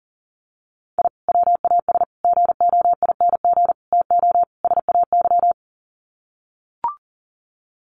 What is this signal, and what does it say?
Morse code, heard as a keyed tone, about 40 wpm, 700 Hz. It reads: I WAS GOING TO SAY
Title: Sets of 5 Words 40wpm